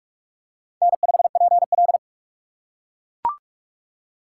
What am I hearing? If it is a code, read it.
Morse code, heard as a keyed tone, about 44 wpm, 700 Hz. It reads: N5PL
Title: USA Call Signs - 4 Characters - 4x Character Spacing - 45wpm